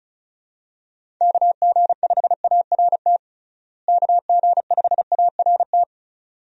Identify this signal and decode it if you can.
Morse code, heard as a keyed tone, about 36 wpm, 700 Hz. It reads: KG5ART KG5ART
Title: USA Call Signs - 6 Characters - Repeated 2x - 35wpm